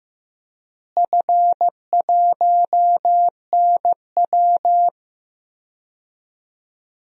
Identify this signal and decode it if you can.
Morse code, heard as a keyed tone, about 15 wpm, 700 Hz. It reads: F1NW
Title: DXCC-100 Call Signs - 4 Characters - Repeated 2x - 15wpm